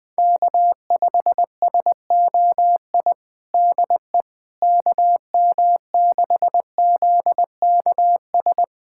Morse code, heard as a keyed tone, about 20 wpm, 700 Hz. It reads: K5SOI DE KM6ZKS
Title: USA Callsign DE Callsign 20wpm